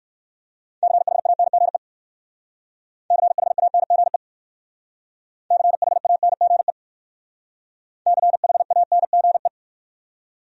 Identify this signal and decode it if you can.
Morse code, heard as a keyed tone, about 56 wpm, 700 Hz. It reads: CHANGE CHANGE CHANGE CHANGE
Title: Speed-Racing Top 300 Words 45wpm